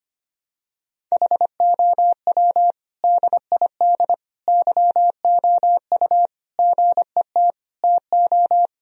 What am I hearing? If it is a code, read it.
Morse code, heard as a keyed tone, about 25 wpm, 700 Hz. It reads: HOW DID YOU GET TO
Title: Sentences from Top 1000 Words - Review Part 3 - 25wpm